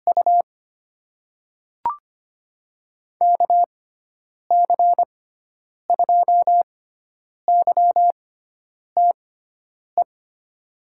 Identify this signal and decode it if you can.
Morse code, heard as a keyed tone, about 25 wpm, spaced as for 9 wpm, 700 Hz. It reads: U KC2YTE
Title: USA Call Signs - 6 Characters - 6x Character Spacing - 25wpm